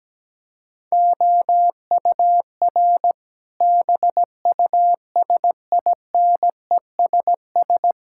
Morse code, heard as a keyed tone, about 17 wpm, 700 Hz. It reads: OUR BUSINESS
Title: Sentences from Top 1000 Words - Review Part 6 - 17wpm